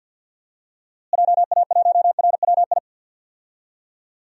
Morse code, heard as a keyed tone, about 53 wpm, 700 Hz. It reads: JA1RWI
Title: USA Call Signs - 6 Characters - Repeated 2x - 50wpm